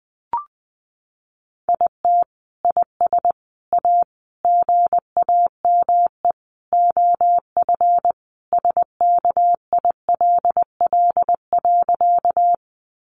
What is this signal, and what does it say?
Morse code, heard as a keyed tone, about 20 wpm, 700 Hz. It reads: IT IS A GAME OF SKILL.
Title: Sentences from Top 1200 Words 20wpm